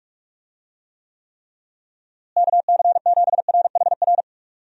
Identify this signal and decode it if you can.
Morse code, heard as a keyed tone, about 41 wpm, 700 Hz. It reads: KC7RHR
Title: USA Call Signs - 6 Characters - Repeated 3x - 45wpm